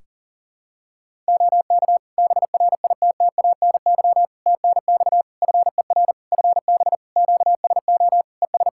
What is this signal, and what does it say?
Morse code, heard as a keyed tone, about 40 wpm, 700 Hz. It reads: OK BRITTANY TNX FER FB QSO ES
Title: Practice QSO - Exchange 4 - Ending - 40wpm